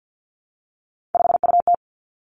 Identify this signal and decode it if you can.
Morse code, heard as a keyed tone, about 50 wpm, 700 Hz. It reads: 5UT
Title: Three Challenging Letter-Number Combinations 50wpm